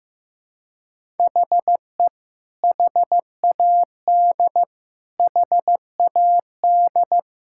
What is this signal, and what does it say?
Morse code, heard as a keyed tone, about 15 wpm, 700 Hz. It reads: HE HAD HAD
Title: Sentences from Top 1000 Words - Review Part 6 - 15wpm